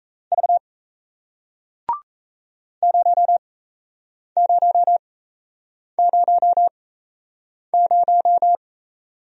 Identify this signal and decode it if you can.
Morse code, heard as a keyed tone, about 38 wpm, spaced as for 8 wpm, 700 Hz. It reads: V 0000